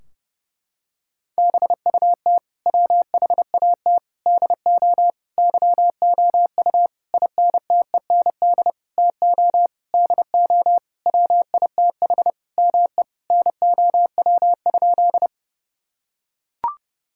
Morse code, heard as a keyed tone, about 32 wpm, 700 Hz. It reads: BUT WHAT DO YOU INTEND TO DO WITH ME NOW?
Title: Sentences from Top 1600 Words 30wpm